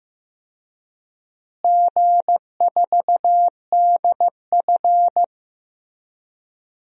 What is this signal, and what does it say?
Morse code, heard as a keyed tone, about 15 wpm, 700 Hz. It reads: G4DF
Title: DXCC-100 Call Signs - 4 Characters - Repeated 2x - 15wpm